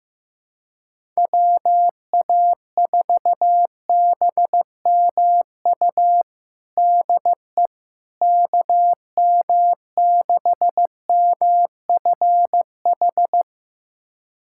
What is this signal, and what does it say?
Morse code, heard as a keyed tone, about 15 wpm, 700 Hz. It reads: WA4BMU DE KM6MFH
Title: USA Callsign DE Callsign 15wpm